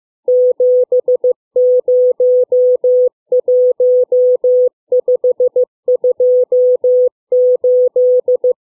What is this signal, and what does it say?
Morse code, heard as a keyed tone, about 15 wpm, 500 Hz. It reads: 701528
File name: CW / letter number groups 2